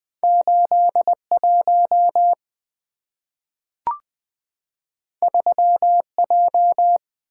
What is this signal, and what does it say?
Morse code, heard as a keyed tone, about 20 wpm, 700 Hz. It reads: 81 3J
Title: Two Letter-Number Combinations - 20wpm